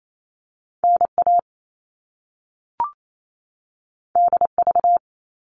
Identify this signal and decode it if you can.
Morse code, heard as a keyed tone, about 28 wpm, 700 Hz. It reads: NA DV
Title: All Cut Numbers - 2-Digit 28wpm